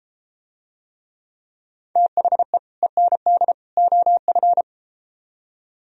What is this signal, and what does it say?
Morse code, heard as a keyed tone, about 32 wpm, 700 Hz. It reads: THE END OF